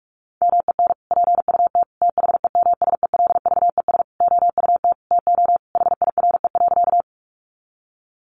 Morse code, heard as a keyed tone, about 44 wpm, 700 Hz. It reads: MEN PUT THEMSELVES OUT TO HIRE.
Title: Sentences from Top 1400 Words 45wpm